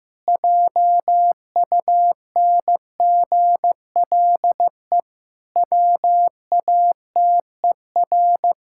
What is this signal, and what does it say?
Morse code, heard as a keyed tone, about 15 wpm, 700 Hz. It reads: JUNGLE WATER
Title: Pangram Challenge v2 - 15wpm